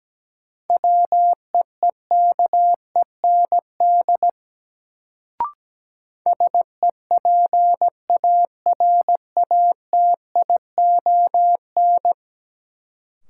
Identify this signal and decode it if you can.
Morse code, heard as a keyed tone, about 17 wpm, 700 Hz. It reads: WEEKEND SEPARATION
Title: Sets of 5 Words - New Word Drill - 17wpm